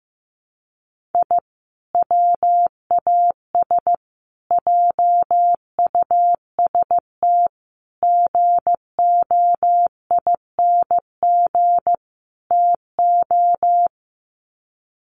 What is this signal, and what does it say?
Morse code, heard as a keyed tone, about 15 wpm, 700 Hz. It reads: I WAS JUST GOING TO
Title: Sets of 5 Words Encore 15wpm